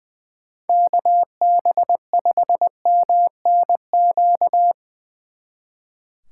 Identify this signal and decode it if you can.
Morse code, heard as a keyed tone, about 22 wpm, 700 Hz. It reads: KB5MNQ
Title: Speed Racing - USA 2x3 Call Signs 20wpm